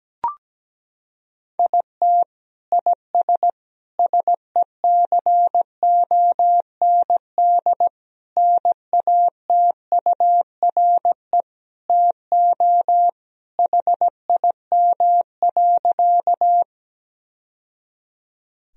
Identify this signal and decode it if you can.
Morse code, heard as a keyed tone, about 17 wpm, 700 Hz. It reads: IT IS SECOND NATURE TO HIM.
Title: Sentences from Top 1000 Words - Review Part 1 - 17wpm